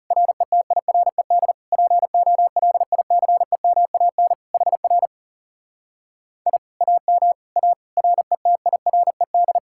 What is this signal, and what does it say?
Morse code, heard as a keyed tone, about 38 wpm, 700 Hz. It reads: RETIRED POLICEMAN HR I AM A RETIRED
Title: Speed-Racing Common QSO Phrases Encore - 30wpm